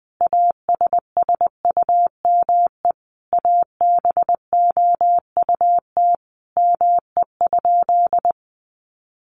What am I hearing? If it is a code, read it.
Morse code, heard as a keyed tone, about 20 wpm, 700 Hz. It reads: ASSUME ABOUT ME?
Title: Sentences from Top 1500 Words 20wpm